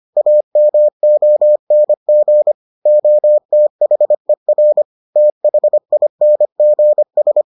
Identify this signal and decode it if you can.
Morse code, heard as a keyed tone, about 25 wpm, 600 Hz. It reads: AMONG OTHER THINGS